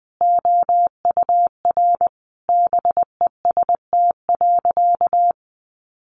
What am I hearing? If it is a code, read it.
Morse code, heard as a keyed tone, about 20 wpm, 700 Hz. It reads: OUR BEST.
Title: Sentences from Top 800 Words 20wpm